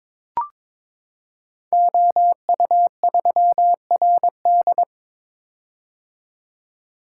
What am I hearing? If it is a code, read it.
Morse code, heard as a keyed tone, about 22 wpm, 700 Hz. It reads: OU3RD
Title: Code Groups - 5 Letter:Numbers - 22wpm